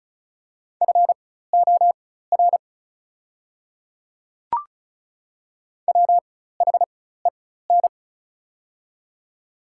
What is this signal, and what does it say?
Morse code, heard as a keyed tone, about 34 wpm, spaced as for 16 wpm, 690 Hz. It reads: FOR WHEN
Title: 4x Character Spacing - Top 300 Words - 35wpm